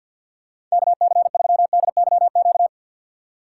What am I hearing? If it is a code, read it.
Morse code, heard as a keyed tone, about 51 wpm, 700 Hz. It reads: KC3DYQ
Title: USA Call Signs - 3x Character Spacing - 50wpm